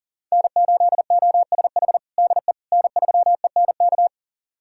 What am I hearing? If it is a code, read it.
Morse code, heard as a keyed tone, about 37 wpm, 700 Hz. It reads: N8OSH DE N3ENK
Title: USA Callsign DE Callsign - 5 Characters - 40wpm